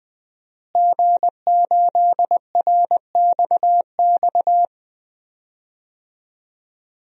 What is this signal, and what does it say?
Morse code, heard as a keyed tone, about 20 wpm, 700 Hz. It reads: G8RXX
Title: DXCC-100 Call Signs - 5 Characters - Repeated 2x - 20wpm